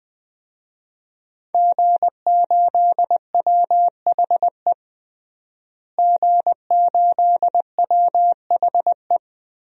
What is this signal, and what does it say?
Morse code, heard as a keyed tone, about 21 wpm, 700 Hz. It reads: G8WHE G8WHE
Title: DXCC-100 Call Signs - 5 Characters - Repeated 2x - 20wpm